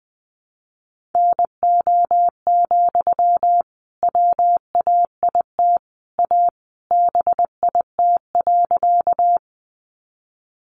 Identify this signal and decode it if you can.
Morse code, heard as a keyed tone, about 20 wpm, 700 Hz. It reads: NO, WAIT A BIT.
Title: Sentences from Top 900 Words 20wpm